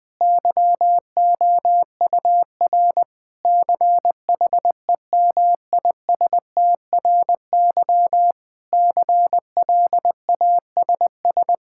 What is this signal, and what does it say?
Morse code, heard as a keyed tone, about 20 wpm, 700 Hz. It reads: YOUR CHEMISTRY CLASS
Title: Sentences from Top 1000 Words - Review Part 6 - 20wpm